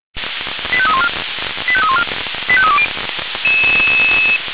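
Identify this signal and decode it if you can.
Five-tone selective call (ZVEI), heard as quick runs of five tones, tones of about 70 ms. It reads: 85314 85314 85320
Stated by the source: ZVEI five-tone selective call received on SDR, 496 channel path.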